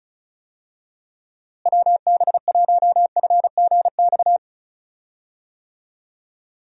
Morse code, heard as a keyed tone, about 36 wpm, 700 Hz. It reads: WB1FGX
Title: USA Call Signs - 6 Characters - Repeated 3x - 35wpm